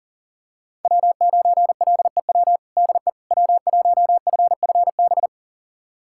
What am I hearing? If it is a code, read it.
Morse code, heard as a keyed tone, about 40 wpm, 700 Hz. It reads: W9LEW DE W1FFB
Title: USA Callsign DE Callsign - 5 Characters - 40wpm